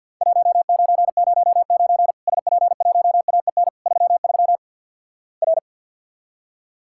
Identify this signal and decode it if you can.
Morse code, heard as a keyed tone, about 49 wpm, 700 Hz. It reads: 1909 SP1AEN 33 R
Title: Worked All Europe DX Contest - QTC Exchange - 50wpm